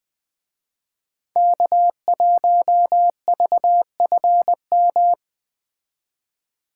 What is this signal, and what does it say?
Morse code, heard as a keyed tone, about 20 wpm, 700 Hz. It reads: K1VFM